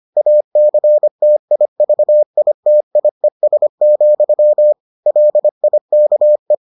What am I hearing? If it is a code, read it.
Morse code, heard as a keyed tone, about 25 wpm, 600 Hz. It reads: ACTIVITIES, LIKE